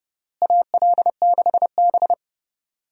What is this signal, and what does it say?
Morse code, heard as a keyed tone, about 32 wpm, 700 Hz. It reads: AL6B
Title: USA Call Signs - 4 Characters - Repeated 3x - 30wpm